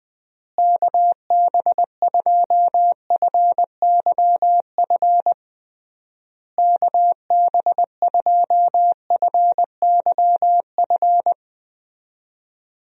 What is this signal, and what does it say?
Morse code, heard as a keyed tone, about 21 wpm, 700 Hz. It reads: KB2FYF KB2FYF
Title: USA Call Signs - 6 Characters - Repeated 3x - 20wpm